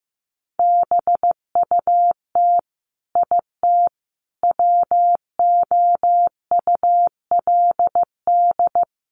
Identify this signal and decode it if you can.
Morse code, heard as a keyed tone, about 15 wpm, 700 Hz. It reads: BUT IT WOULD
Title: Sentences from Top 1400 Words 15wpm